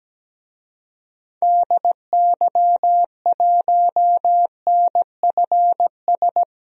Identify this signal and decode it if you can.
Morse code, heard as a keyed tone, about 17 wpm, 700 Hz. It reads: DY1NFS
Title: DXCC-100 Call Signs - 6 Characters - 17wpm